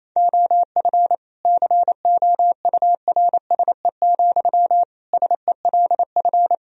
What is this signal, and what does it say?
Morse code, heard as a keyed tone, about 28 wpm, 700 Hz. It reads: OF COURSE, SELF
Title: Sentences from Top 1000 Words - Review Part 2 - 28wpm